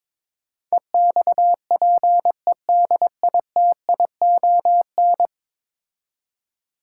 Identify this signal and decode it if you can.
Morse code, heard as a keyed tone, about 21 wpm, 700 Hz. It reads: EXPEDITION
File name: practice / 22wpm / 10 Letter Words - First 3 Letters Given - 22wpm